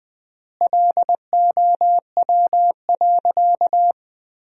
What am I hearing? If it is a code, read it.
Morse code, heard as a keyed tone, about 20 wpm, 700 Hz. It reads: LOW.
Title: 3x Word-Spacing - Sentences from Top 500 Words - 20wpm